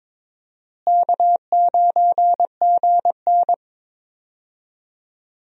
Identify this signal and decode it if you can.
Morse code, heard as a keyed tone, about 22 wpm, 700 Hz. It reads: K9GN
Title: USA Call Signs - 4 Characters - Repeated 3x - 22wpm